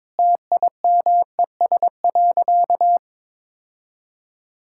Morse code, heard as a keyed tone, about 22 wpm, 700 Hz. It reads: TIMES.
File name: Difficult Sentences with Hints - 22wpm